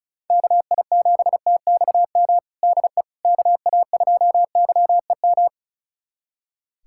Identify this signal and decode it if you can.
Morse code, heard as a keyed tone, about 36 wpm, 700 Hz. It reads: KI7TXM DE KA2YEM
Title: USA Callsign DE Callsign - 6 Characters - 35wpm